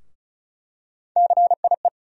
Morse code, heard as a keyed tone, about 34 wpm, 700 Hz. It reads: CIE
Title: USA Call Sign Postfixes - 3 Letters 35wpm